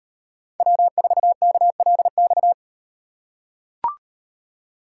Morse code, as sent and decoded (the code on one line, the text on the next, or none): .-- ....- -.- .-.. -..-
W4KLX